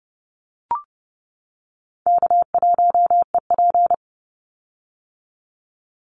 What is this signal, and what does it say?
Morse code, heard as a keyed tone, about 30 wpm, 700 Hz. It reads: K1EP
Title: Easy Call Signs - 30wpm